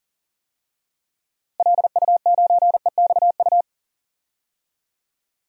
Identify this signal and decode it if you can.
Morse code, heard as a keyed tone, about 40 wpm, 700 Hz. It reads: LU9EXU
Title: DXCC-100 Call Signs - 6 Characters - Repeated 3x - 40wpm